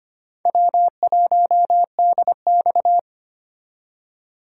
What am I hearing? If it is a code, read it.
Morse code, heard as a keyed tone, about 25 wpm, 700 Hz. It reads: W1DX
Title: K1USN SST QSO Practice - 25wpm